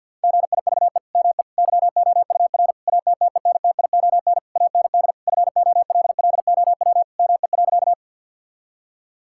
Morse code, heard as a keyed tone, about 50 wpm, 700 Hz. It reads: GIVE ME YOUR ATTENTION AND FOLLOW ME.
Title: Sentences from Top 1000 Words - Review Part 6 - 50wpm